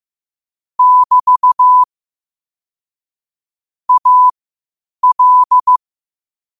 Morse code, heard as a keyed tone, about 15 wpm, spaced as for 8 wpm, 1000 Hz. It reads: <BT> AL
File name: qotdfiles / 8wpm